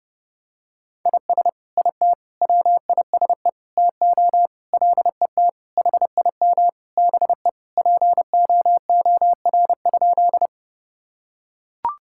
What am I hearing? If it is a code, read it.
Morse code, heard as a keyed tone, about 32 wpm, 700 Hz. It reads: IS IT WISE TO LET HIM BE POOR?
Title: Sentences from Top 1600 Words 30wpm